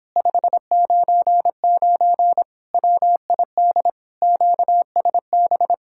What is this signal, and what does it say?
Morse code, heard as a keyed tone, about 29 wpm, 700 Hz. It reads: 599 WID QSB